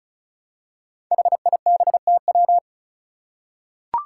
Morse code, heard as a keyed tone, about 34 wpm, 700 Hz. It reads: HIBTW
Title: Five Challenging Letter-Number Combinations 35wpm